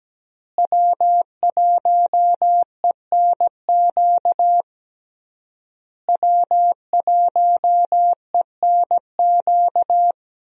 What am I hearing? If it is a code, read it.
Morse code, heard as a keyed tone, about 17 wpm, 700 Hz. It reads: W1ENQ W1ENQ
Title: USA Call Signs - 5 Characters - Repeated 3x - 17wpm